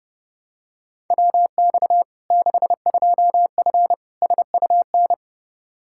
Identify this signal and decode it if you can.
Morse code, heard as a keyed tone, about 30 wpm, 700 Hz. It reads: WX 62F SUN
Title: Practice QSO - Exchange 2 - Info - 30wpm